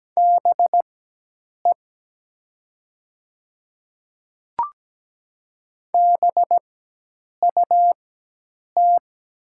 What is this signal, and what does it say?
Morse code, heard as a keyed tone, about 17 wpm, spaced as for 8 wpm, 700 Hz. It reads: BE BUT
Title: 4x Character Spacing - Top 200 Words - 17wpm